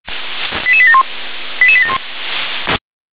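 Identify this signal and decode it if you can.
Five-tone selective call (ZVEI), heard as quick runs of five tones, tones of about 70 ms. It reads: 88871 88871
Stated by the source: ZVEI five-tone call 88641 received by SDR.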